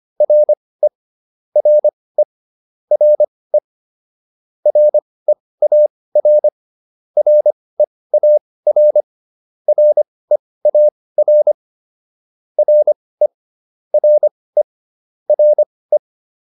Morse code, heard as a keyed tone, about 25 wpm, spaced as for 18 wpm, 600 Hz. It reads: RE RE RE REAR REAR REAR RE RE RE